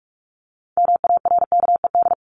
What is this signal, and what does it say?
Morse code, heard as a keyed tone, about 44 wpm, 700 Hz. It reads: MARKED
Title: ED Suffix - Top 100 Words with Base Words 45wpm